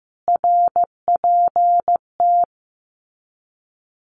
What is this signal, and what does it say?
Morse code, heard as a keyed tone, about 15 wpm, 700 Hz. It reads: RPT
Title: Abbreviations Repeated 3x - 15wpm